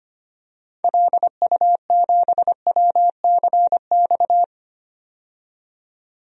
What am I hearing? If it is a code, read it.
Morse code, heard as a keyed tone, about 25 wpm, 700 Hz. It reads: LU7WCX
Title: DXCC-100 Call Signs - 6 Characters - 25wpm